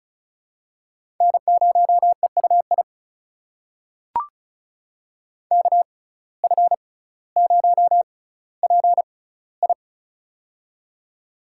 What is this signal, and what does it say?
Morse code, heard as a keyed tone, about 36 wpm, spaced as for 18 wpm, 700 Hz. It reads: N0EUI KF0PI
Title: USA Call Signs - 5 Characters - 6x Character Spacing - 35wpm